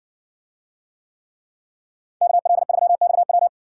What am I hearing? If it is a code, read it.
Morse code, heard as a keyed tone, about 60 wpm, 700 Hz. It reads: KL3XW